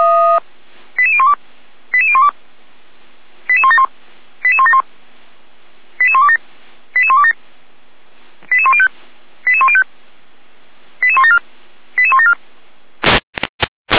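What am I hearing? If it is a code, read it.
Five-tone selective call (ZVEI), heard as quick runs of five tones, tones of about 70 ms. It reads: 88012 88012 88171 88171 88127 88127 88105 88105 88174 88174
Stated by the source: ZVEI five-tone selective call received on an SDR by the Feuerwehr Orlamünde server.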